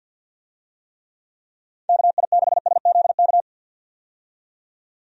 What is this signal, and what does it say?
Morse code, heard as a keyed tone, about 53 wpm, 700 Hz. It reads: KI6SZK
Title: USA Call Signs - 6 Characters - 50wpm